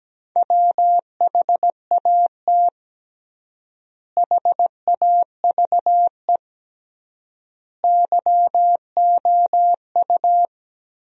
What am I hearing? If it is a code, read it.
Morse code, heard as a keyed tone, about 17 wpm, 700 Hz. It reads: WHAT HAVE YOU
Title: 3x Word-Spacing - Sentences from Top 100 Words - 17wpm